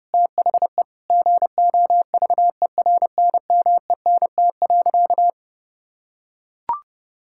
Morse code, heard as a keyed tone, about 32 wpm, 700 Hz. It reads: THE GOVERNMENT.
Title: Sentences from Top 1600 Words 30wpm